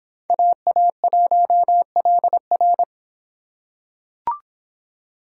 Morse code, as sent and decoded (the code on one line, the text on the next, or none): .- .- .---- .-.. .-.
AA1LR